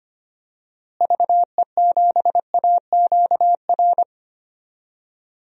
Morse code, heard as a keyed tone, about 25 wpm, 700 Hz. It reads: VE7AQR
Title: DXCC-100 Call Signs - 6 Characters - Repeated 2x - 25wpm